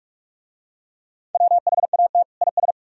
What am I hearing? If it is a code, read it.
Morse code, heard as a keyed tone, about 43 wpm, 700 Hz. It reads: WHAT IS